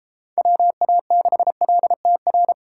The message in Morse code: WA6LTR

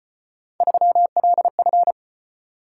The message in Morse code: 3LF

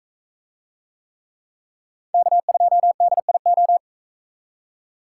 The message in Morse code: K2DIO

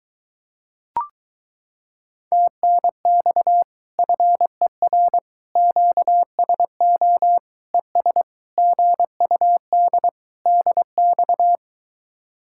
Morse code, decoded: TNX FER QSO ES GUD DX

